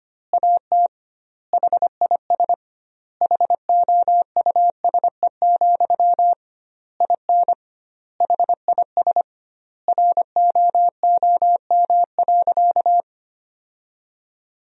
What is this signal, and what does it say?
Morse code, heard as a keyed tone, about 25 wpm, 700 Hz. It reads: AT HIS HOUSE, IN HIS ROOM.
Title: 2x Word-Spacing - Sentences from Top 300 Words - 25wpm